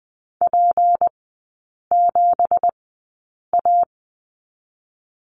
Morse code, heard as a keyed tone, about 20 wpm, spaced as for 8 wpm, 700 Hz. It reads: P7A
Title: 212 Lesson 34e - Number 8 - Three Characters 20wpm